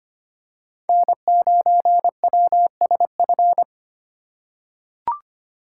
Morse code, heard as a keyed tone, about 25 wpm, 700 Hz. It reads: N9WSF
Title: USA Call Signs - 5 Characters - 4x Character Spacing - 25wpm